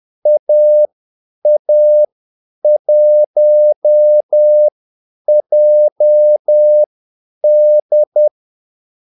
Morse code, heard as a keyed tone, about 10 wpm, spaced as for 8 wpm, 600 Hz. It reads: AA1JD
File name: Callsigns-10-8wpm